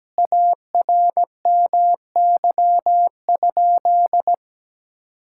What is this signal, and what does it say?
Morse code, heard as a keyed tone, about 17 wpm, 700 Hz. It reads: ARMY?
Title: Sentences from Top 1000 Words - Review Part 5 - 17wpm